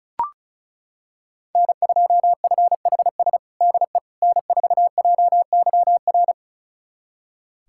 Morse code, heard as a keyed tone, about 36 wpm, 700 Hz. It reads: N2FHS DE N4JYR
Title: USA Callsign DE Callsign - 5 Characters - 35wpm